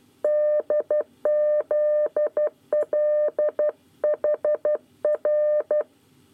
Morse code, heard as a keyed tone, about 12 wpm, 600 Hz. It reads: DZLHR
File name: 9 lezione cw 1